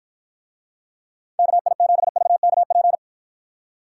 Morse code, heard as a keyed tone, about 54 wpm, 700 Hz. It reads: KI7VKP